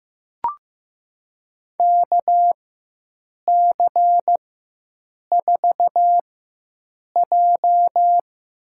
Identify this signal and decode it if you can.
Morse code, heard as a keyed tone, about 15 wpm, spaced as for 7 wpm, 700 Hz. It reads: KC4J
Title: USA Call Signs - 4x Character Spacing - 15wpm